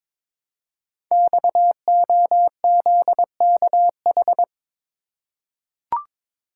Morse code, heard as a keyed tone, about 23 wpm, 700 Hz. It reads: XOZKH
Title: Code Groups - 5 Letter:Numbers - 22wpm